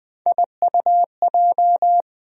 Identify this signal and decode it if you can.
Morse code, heard as a keyed tone, about 20 wpm, 700 Hz. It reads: IUJ
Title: USA Call Sign Postfixes - 3 Letters 20wpm